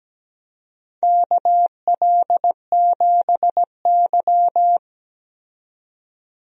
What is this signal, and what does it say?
Morse code, heard as a keyed tone, about 17 wpm, 700 Hz. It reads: KL7Y
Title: DXCC-100 Call Signs - 4 Characters - 17wpm